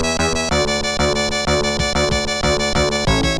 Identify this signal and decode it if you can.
There is no code.